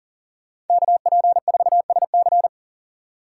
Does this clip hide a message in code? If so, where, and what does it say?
Yes: Morse code from 0.7 to 2.5 s, heard as a keyed tone, about 53 wpm, 700 Hz. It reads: KP4SC